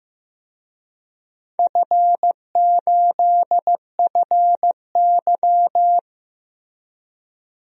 Morse code, heard as a keyed tone, about 15 wpm, 700 Hz. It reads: F8FY
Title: DXCC-100 Call Signs - 4 Characters - 15wpm